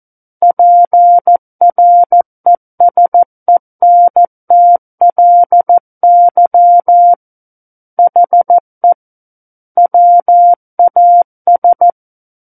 Morse code, heard as a keyed tone, about 14 wpm, 700 Hz. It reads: PRESENTLY HE WAS
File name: Book One / 05-Chapter 5 - 14 WPM